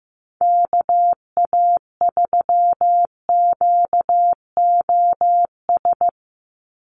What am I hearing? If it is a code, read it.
Morse code, heard as a keyed tone, about 15 wpm, 700 Hz. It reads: KA3QOS
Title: USA Call Signs 15wpm